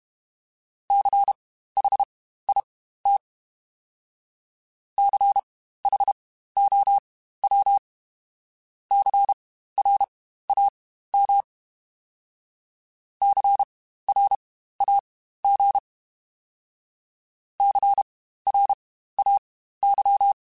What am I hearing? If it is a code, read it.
Morse code, heard as a keyed tone, about 32 wpm, spaced as for 15 wpm, 800 Hz. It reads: CHIT CHOW CRAM CRAG CRAY
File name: Spaced Alphas / Spaced Alphas mp3s / Cs sp @30